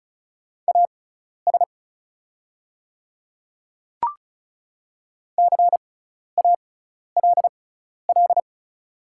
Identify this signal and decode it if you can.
Morse code, heard as a keyed tone, about 36 wpm, spaced as for 12 wpm, 700 Hz. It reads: AS CALL
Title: 6x Character Spacing - Top 200 Words - 35wpm